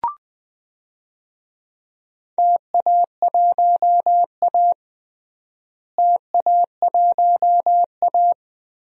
Morse code, heard as a keyed tone, about 21 wpm, 700 Hz. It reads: TA1A TA1A
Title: DXCC-100 Call Signs - 4 Characters - Repeated 2x - 20wpm